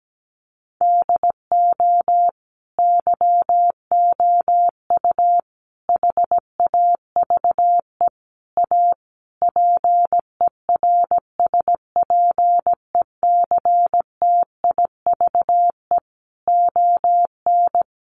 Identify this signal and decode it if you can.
Morse code, heard as a keyed tone, about 17 wpm, 700 Hz. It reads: DO YOU HAVE A PERSPECTIVE ON
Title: Sentences from Top 1200 Words 17wpm